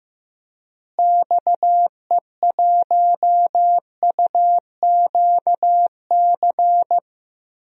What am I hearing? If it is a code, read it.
Morse code, heard as a keyed tone, about 15 wpm, 700 Hz. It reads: XE1UQC
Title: DXCC-100 Call Signs - 6 Characters - Repeated 3x - 15wpm